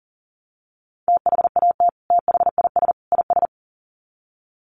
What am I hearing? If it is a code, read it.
Morse code, heard as a keyed tone, about 40 wpm, 700 Hz. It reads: THAT THIS IS
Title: Sets of 3 Words 40wpm